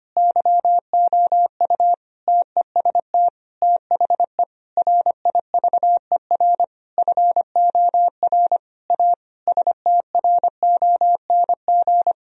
Morse code, heard as a keyed tone, about 25 wpm, 700 Hz. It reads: YOU TEST THE RIVER FOR A STRONG
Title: Sentences from Top 1000 Words - Review Part 4 - 25wpm